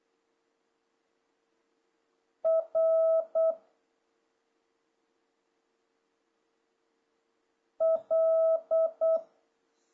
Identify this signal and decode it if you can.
Morse code, heard as a keyed tone, about 8 wpm, 650 Hz. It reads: R L
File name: jh8cbh / 2026cw